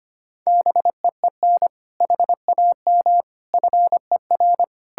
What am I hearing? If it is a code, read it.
Morse code, heard as a keyed tone, about 31 wpm, 700 Hz. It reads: BEEN HAM FER
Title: Speed-Racing Common QSO Phrases - 25wpm